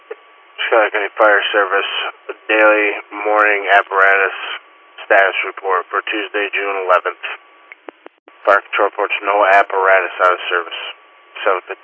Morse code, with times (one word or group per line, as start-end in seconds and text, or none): none